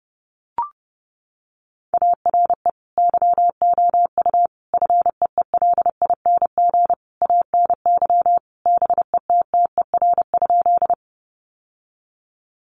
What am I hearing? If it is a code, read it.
Morse code, heard as a keyed tone, about 30 wpm, 700 Hz. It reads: ARE YOU FEELING ANY BETTER?
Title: Sentences from Top 1200 Words 30wpm